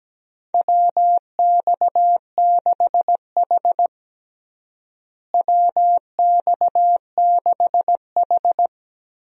Morse code, heard as a keyed tone, about 17 wpm, 700 Hz. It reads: WX6H WX6H
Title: USA Call Signs - 4 Characters - Repeated 2x - 17wpm